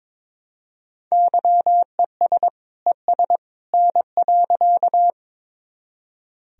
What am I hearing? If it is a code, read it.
Morse code, heard as a keyed tone, about 22 wpm, 700 Hz. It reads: YES ES N.
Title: Sentences from Top 500 Words and Abbreviations 22wpm